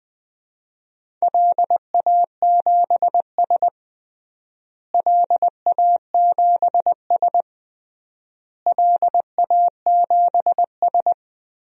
Morse code, heard as a keyed tone, about 21 wpm, 700 Hz. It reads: LA7S LA7S LA7S
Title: DXCC-100 Call Signs - 4 Characters - Repeated 3x - 20wpm